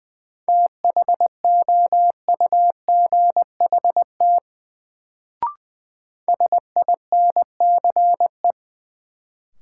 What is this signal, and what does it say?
Morse code, heard as a keyed tone, about 20 wpm, 700 Hz. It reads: THOUGHT SINCE
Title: Sets of 2 Words - New Word Drill - 20wpm